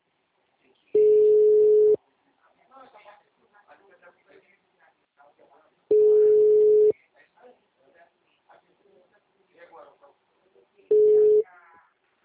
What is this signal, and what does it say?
Ringback tone (Europe, 425 Hz), with speech.